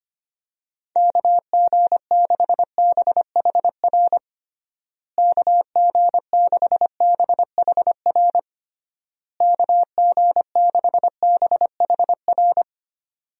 Morse code, heard as a keyed tone, about 25 wpm, 700 Hz. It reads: KG6BHR KG6BHR KG6BHR
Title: USA Call Signs - 6 Characters - Repeated 3x - 25wpm